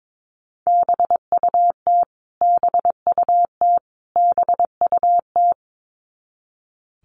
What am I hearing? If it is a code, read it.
Morse code, heard as a keyed tone, about 22 wpm, 700 Hz. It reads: BUT BUT BUT
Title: Top 300 Words Repeated 22wpm